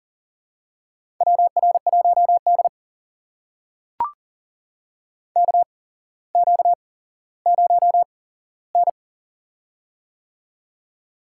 Morse code, heard as a keyed tone, about 42 wpm, spaced as for 17 wpm, 700 Hz. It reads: WR1D KQ0N
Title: USA Call Signs - 4 Characters - 8x Character Spacing - 40wpm